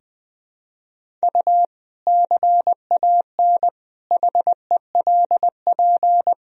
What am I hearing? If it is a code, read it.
Morse code, heard as a keyed tone, about 20 wpm, 700 Hz. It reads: U CAN HELP
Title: Sentences from Top 200 Words and Abbreviations 20wpm